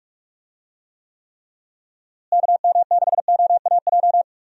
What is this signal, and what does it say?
Morse code, heard as a keyed tone, about 46 wpm, 700 Hz. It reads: KM6OAJ
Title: USA Call Signs - 6 Characters - Repeated 3x - 45wpm